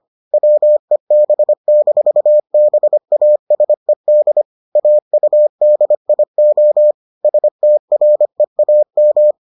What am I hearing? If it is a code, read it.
Morse code, heard as a keyed tone, about 25 wpm, 600 Hz. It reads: WEB-BASED AUDIO STREAM